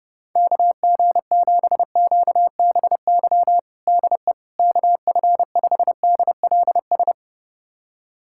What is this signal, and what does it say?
Morse code, heard as a keyed tone, about 29 wpm, 700 Hz. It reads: KG7QBY DE KF5DLS
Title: USA Callsign DE Callsign - 6 Characters - 30wpm